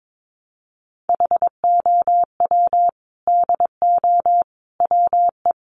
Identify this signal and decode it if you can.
Morse code, heard as a keyed tone, about 22 wpm, 700 Hz. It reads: HOW DO WE
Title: A Sound Of Thunder 22wpm